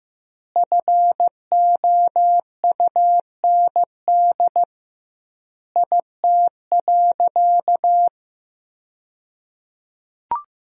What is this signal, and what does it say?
Morse code, heard as a keyed tone, about 15 wpm, 700 Hz. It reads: FOUND IT.
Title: 2x Word-Spacing - Sentences from Top 300 Words - 15wpm